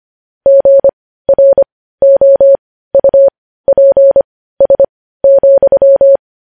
Morse code, heard as a keyed tone, about 25 wpm, spaced as for 15 wpm, 550 Hz. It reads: GROUPS,
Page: news-headlines-in-morse-code-at-15-wpm